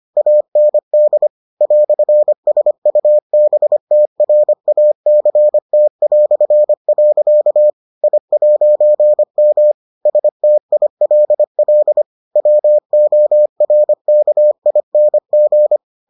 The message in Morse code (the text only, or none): AND "SUBTRACT". I'M STILL WORKING